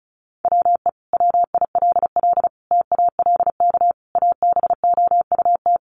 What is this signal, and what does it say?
Morse code, heard as a keyed tone, about 34 wpm, 700 Hz. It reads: WE WILL TALK ABOUT
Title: Sentences from Top 400 Words 35wpm